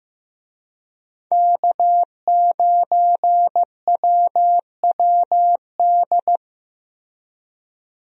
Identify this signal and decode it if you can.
Morse code, heard as a keyed tone, about 15 wpm, 700 Hz. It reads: K9WWD
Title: USA Call Signs - 5 Characters - Repeated 3x - 15wpm